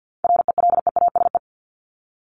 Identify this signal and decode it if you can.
Morse code, heard as a keyed tone, about 50 wpm, 700 Hz. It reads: RELEASE
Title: Top 1001-1250 Words 50wpm